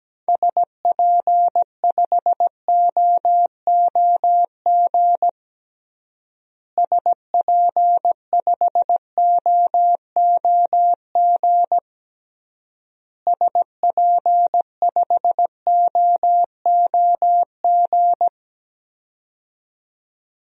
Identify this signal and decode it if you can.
Morse code, heard as a keyed tone, about 17 wpm, 700 Hz. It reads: SP5OOG SP5OOG SP5OOG
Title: DXCC-100 Call Signs - 6 Characters - Repeated 3x - 17wpm